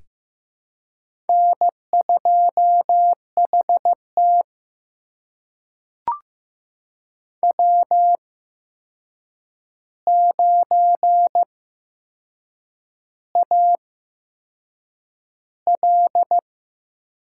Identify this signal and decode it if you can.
Morse code, heard as a keyed tone, about 15 wpm, spaced as for 6 wpm, 700 Hz. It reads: N2HT W9AL
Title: USA Call Signs - 4 Characters - 8x Character Spacing - 15wpm